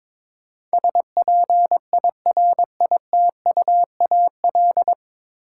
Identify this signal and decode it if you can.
Morse code, heard as a keyed tone, about 22 wpm, 700 Hz. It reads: SPIRITUAL
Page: Morse Code Ninja practice 22wpm